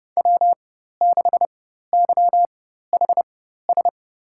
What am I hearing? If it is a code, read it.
Morse code, heard as a keyed tone, about 31 wpm, spaced as for 14 wpm, 700 Hz. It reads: W6YHS
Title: USA Call Signs - 5 Characters - 4x Character Spacing - 30wpm